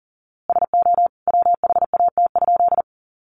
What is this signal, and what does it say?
Morse code, heard as a keyed tone, about 40 wpm, 700 Hz. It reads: SO WHAT?
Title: A Sound Of Thunder 40wpm